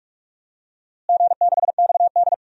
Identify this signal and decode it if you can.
Morse code, heard as a keyed tone, about 46 wpm, 700 Hz. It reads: G6XD
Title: DXCC-100 Call Signs - 4 Characters - Repeated 3x - 45wpm